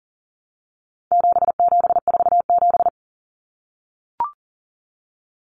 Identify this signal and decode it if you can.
Morse code, heard as a keyed tone, about 40 wpm, 700 Hz. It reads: 7747
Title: Common Cut Numbers - 4-Digit 40wpm